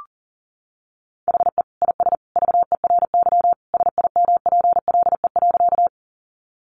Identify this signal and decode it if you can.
Morse code, heard as a keyed tone, about 40 wpm, 700 Hz. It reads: HE IS VERY SIMPLE.
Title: Sentences from Top 400 Words 40wpm